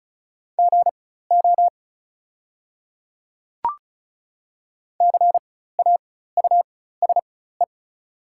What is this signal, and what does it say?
Morse code, heard as a keyed tone, about 35 wpm, spaced as for 16 wpm, 700 Hz. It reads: GO CAUSE
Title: 4x Character Spacing - Top 200 Words - 35wpm